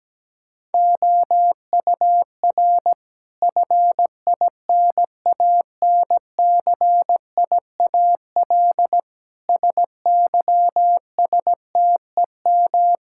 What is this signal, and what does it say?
Morse code, heard as a keyed tone, about 17 wpm, 700 Hz. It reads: OUR FINANCIAL SYSTEM
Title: Sentences from Top 1000 Words - Review Part 5 - 17wpm